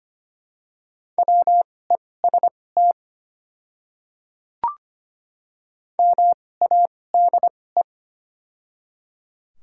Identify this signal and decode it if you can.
Morse code, heard as a keyed tone, about 25 wpm, spaced as for 18 wpm, 700 Hz. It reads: WEST MADE
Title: 2x Character Spacing - Top 500 Words - 25wpm